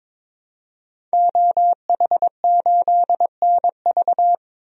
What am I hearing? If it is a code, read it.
Morse code, heard as a keyed tone, about 22 wpm, 700 Hz. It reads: OH8NV
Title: DXCC-100 Call Signs - 5 Characters - Repeated 3x - 22wpm